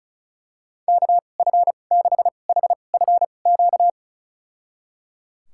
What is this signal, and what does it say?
Morse code, heard as a keyed tone, about 36 wpm, spaced as for 25 wpm, 700 Hz. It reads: KF6HFQ
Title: USA Call Signs - 6 Characters - 2x Character Spacing - 35wpm